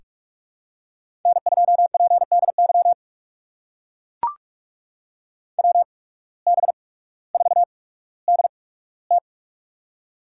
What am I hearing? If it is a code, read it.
Morse code, heard as a keyed tone, about 44 wpm, spaced as for 19 wpm, 690 Hz. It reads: N2PDY WB4DT
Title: USA Call Signs - 5 Characters - 8x Character Spacing - 45wpm